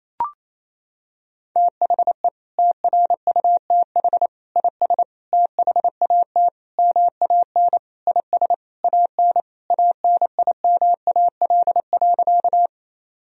Morse code, heard as a keyed tone, about 28 wpm, 700 Hz. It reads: THE TRUTH IS THAT MAN IS AN ANIMAL.
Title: Sentences from Top 1000 Words - Review Part 1 - 28wpm